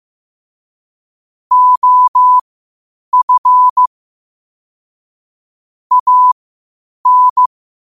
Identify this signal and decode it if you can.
Morse code, heard as a keyed tone, about 15 wpm, spaced as for 8 wpm, 1000 Hz. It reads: OF AN
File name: qotdfiles / 8wpm